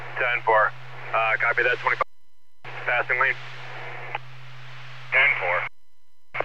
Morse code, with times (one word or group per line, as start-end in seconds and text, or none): none